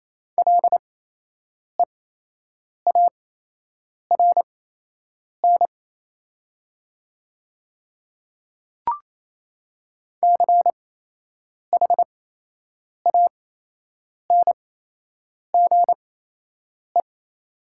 Morse code, heard as a keyed tone, about 28 wpm, spaced as for 8 wpm, 700 Hz. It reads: LEARN CHANGE
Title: 8x Character Spacing - Top 500 Words - 28wpm